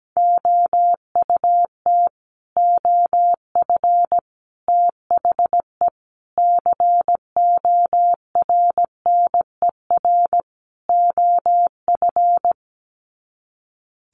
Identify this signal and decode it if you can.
Morse code, heard as a keyed tone, about 17 wpm, 700 Hz. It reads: OUT OF THE CORNER OF
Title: Sets of 5 Words 17wpm